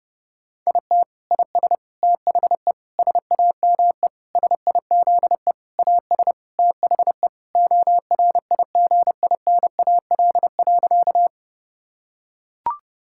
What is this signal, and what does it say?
Morse code, heard as a keyed tone, about 30 wpm, 700 Hz. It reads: IT IS THE SAME SIZE AS THE ORIGINAL.
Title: Sentences from Top 1000 Words - Review Part 6 - 30wpm